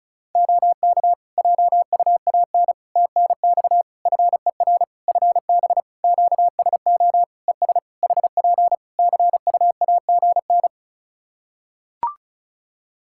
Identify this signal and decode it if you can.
Morse code, heard as a keyed tone, about 36 wpm, 700 Hz. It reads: OK JUAN TNX FER FB QSO ES HP CUAGN
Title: Practice QSO - Exchange 4 - Ending - 35wpm